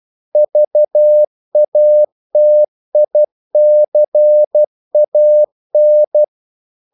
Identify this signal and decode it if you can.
Morse code, heard as a keyed tone, about 12 wpm, 600 Hz. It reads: VATICAN
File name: CW / Practice / NationalCapitals12